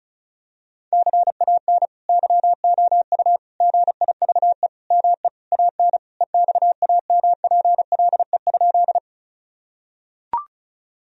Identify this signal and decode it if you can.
Morse code, heard as a keyed tone, about 34 wpm, 700 Hz. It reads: CAN YOU GIVE ME AN EXAMPLE?
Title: Sentences from Top 1000 Words - Review Part 1 - 35wpm